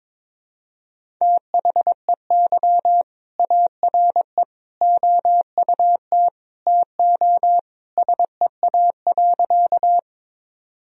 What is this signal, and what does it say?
Morse code, heard as a keyed tone, about 22 wpm, 700 Hz. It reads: THEY ARE OUT TO SEA.
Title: Sentences from Top 1000 Words - Review Part 1 - 22wpm